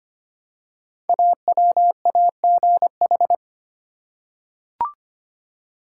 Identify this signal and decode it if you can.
Morse code, heard as a keyed tone, about 25 wpm, 700 Hz. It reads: AWAGH
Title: Code Groups - 5 Letter:Numbers - 25wpm